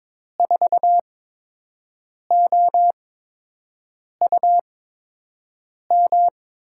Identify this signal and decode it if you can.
Morse code, heard as a keyed tone, about 22 wpm, spaced as for 6 wpm, 700 Hz. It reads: 4OUM
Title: USA Call Signs - 5 Characters - 8x Character Spacing - 22wpm